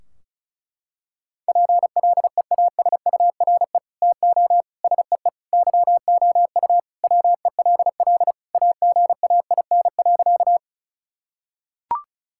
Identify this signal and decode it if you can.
Morse code, heard as a keyed tone, about 36 wpm, 700 Hz. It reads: PLEASURE TO SEE YOU WELL AGAIN.
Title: Sentences from Top 1600 Words 35wpm